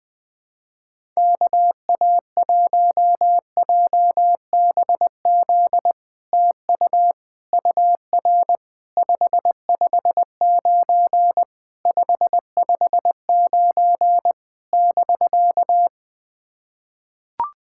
Morse code, heard as a keyed tone, about 20 wpm, 700 Hz. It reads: KA1JBZ TU UR 559 559 <BK>
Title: SOTA QSO Practice 20wpm